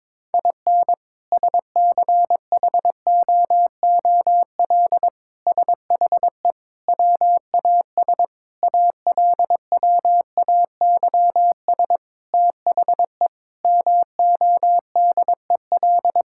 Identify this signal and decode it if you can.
Morse code, heard as a keyed tone, about 22 wpm, 700 Hz. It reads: IN SCHOOL SHE WAS ALWAYS THE MODEL
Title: Sentences from Top 1000 Words - Review Part 5 - 22wpm